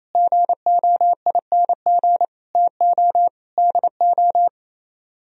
Morse code, heard as a keyed tone, about 36 wpm, 700 Hz. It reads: GOING TO DO